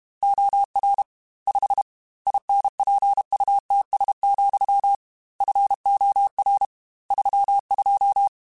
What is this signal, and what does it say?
Morse code, heard as a keyed tone, about 32 wpm, 800 Hz. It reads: OR 5 INPUTS, FOR 32
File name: Technical Subjects / Technical Subjects-mp3s / AND Gate @30